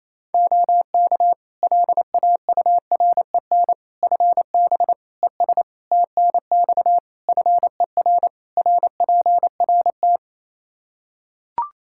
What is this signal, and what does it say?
Morse code, heard as a keyed tone, about 28 wpm, 700 Hz. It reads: OK LAUREN FB ES TNX FER RPRT
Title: Practice QSO - Exchange 2 - Info - 28wpm